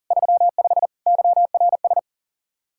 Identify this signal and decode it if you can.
Morse code, heard as a keyed tone, about 37 wpm, 700 Hz. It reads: 35 YRS
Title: Practice QSO - Exchange 3 - More Info - 40wpm